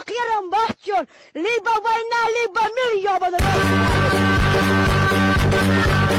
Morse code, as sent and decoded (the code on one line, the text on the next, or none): none